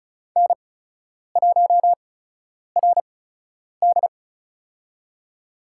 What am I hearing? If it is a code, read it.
Morse code, heard as a keyed tone, about 36 wpm, spaced as for 10 wpm, 700 Hz. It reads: N1RD
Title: USA Call Signs - 8x Character Spacing - 35wpm